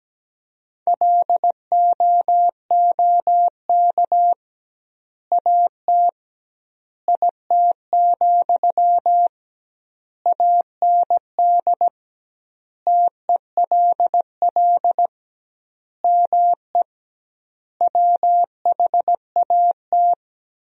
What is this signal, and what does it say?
Morse code, heard as a keyed tone, about 17 wpm, 700 Hz. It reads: LOOK AT IT, AND TELL ME WHAT
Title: 2x Word-Spacing - Sentences from Top 200 Words - 17wpm